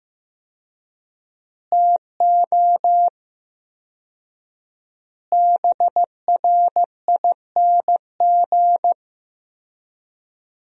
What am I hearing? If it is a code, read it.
Morse code, heard as a keyed tone, about 15 wpm, 700 Hz. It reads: TO BRING
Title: 4x Word-Spacing - Sentences from Top 500 Words - 15wpm